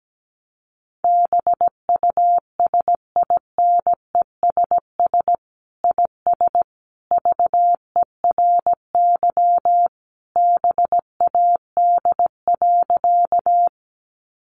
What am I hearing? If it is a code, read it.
Morse code, heard as a keyed tone, about 17 wpm, 700 Hz. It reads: BUSINESS IS VERY BAD.
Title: Sentences from Top 700 Words 17wpm